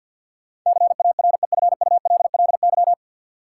Morse code, heard as a keyed tone, about 53 wpm, 700 Hz. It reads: CAREFULLY
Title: Sentences from Top 1300 Words - New Word Drill - 50wpm